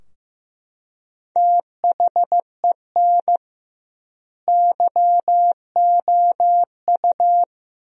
Morse code, heard as a keyed tone, about 16 wpm, 700 Hz. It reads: THEN YOU
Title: 2x Word-Spacing - Sentences from Top 400 Words - 15wpm